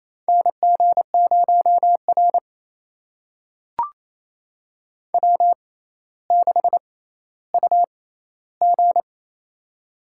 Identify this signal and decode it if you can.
Morse code, heard as a keyed tone, about 28 wpm, spaced as for 14 wpm, 700 Hz. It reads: NG0R W6UG
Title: USA Call Signs - 6x Character Spacing - 28wpm